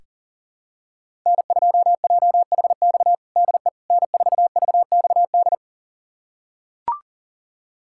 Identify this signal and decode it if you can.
Morse code, heard as a keyed tone, about 37 wpm, 700 Hz. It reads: N2JHX DE N4VXD
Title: USA Callsign DE Callsign - 5 Characters - 40wpm